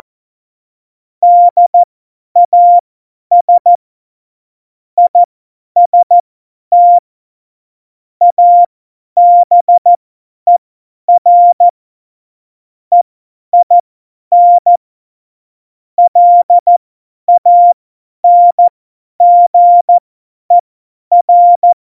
Morse code, heard as a keyed tone, about 14 wpm, spaced as for 10 wpm, 700 Hz. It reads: DAS IST ABER EIN LANGER
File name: Morsekurs / Witze / Witz 21 Juli 2020, 50 F70